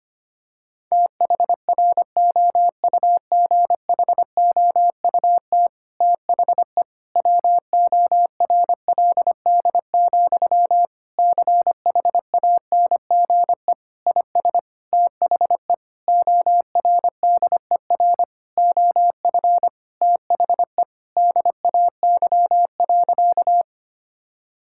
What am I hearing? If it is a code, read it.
Morse code, heard as a keyed tone, about 25 wpm, 700 Hz. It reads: THROUGHOUT THE WORLD, CHANGE IS THE ORDER OF THE DAY.
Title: Sentences from Top 1000 Words - Review Part 5 - 25wpm